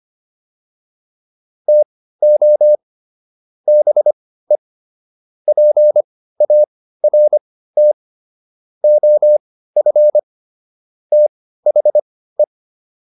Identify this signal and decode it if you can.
Morse code, heard as a keyed tone, about 25 wpm, spaced as for 15 wpm, 600 Hz. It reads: TO BE PART OF THE